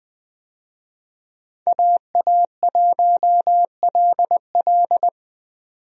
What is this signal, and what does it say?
Morse code, heard as a keyed tone, about 20 wpm, 700 Hz. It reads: AA1LL
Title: USA Call Signs - 5 Characters - 20wpm